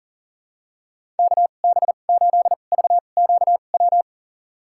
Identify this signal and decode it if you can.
Morse code, heard as a keyed tone, about 40 wpm, spaced as for 29 wpm, 700 Hz. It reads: KB8VQW
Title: USA Call Signs - 2x Character Spacing - 40wpm